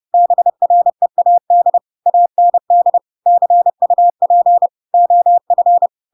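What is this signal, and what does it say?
Morse code, heard as a keyed tone, about 30 wpm, 700 Hz. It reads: BREAD AND CUP OF
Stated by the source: Chapter 2 of The Adventures of Pinocchio read as a book in Morse code (CW) at 30 wpm, by ik1hge.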